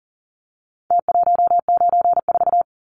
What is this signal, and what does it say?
Morse code, heard as a keyed tone, about 40 wpm, 700 Hz. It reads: T194
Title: Numbers - Contest Style - 4-Digit 40wpm